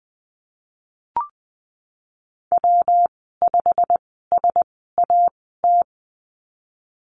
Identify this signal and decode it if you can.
Morse code, heard as a keyed tone, about 20 wpm, spaced as for 15 wpm, 700 Hz. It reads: W5SAT
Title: 147 Lesson 24g - Letter y - Call Signs 20wpm